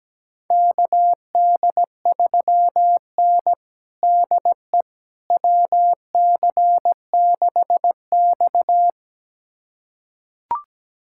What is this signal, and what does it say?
Morse code, heard as a keyed tone, about 17 wpm, 700 Hz. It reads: KD3N DE WC6X
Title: USA Callsign DE Callsign - 4 Characters - 17wpm